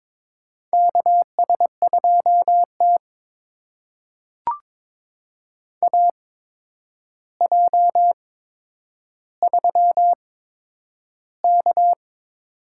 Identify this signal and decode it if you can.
Morse code, heard as a keyed tone, about 22 wpm, spaced as for 9 wpm, 700 Hz. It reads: KS2T AJ3K
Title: USA Call Signs - 4 Characters - 8x Character Spacing - 22wpm